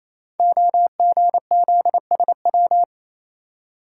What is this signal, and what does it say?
Morse code, heard as a keyed tone, about 28 wpm, 700 Hz. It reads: OGZSW
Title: Code Groups - 5 Letter:Numbers - 28wpm